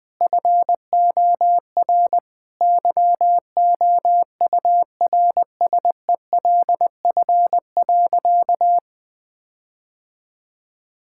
Morse code, heard as a keyed tone, about 20 wpm, 700 Hz. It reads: FOR YOURSELF.
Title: Sentences with 8-Word Phrases - 20wpm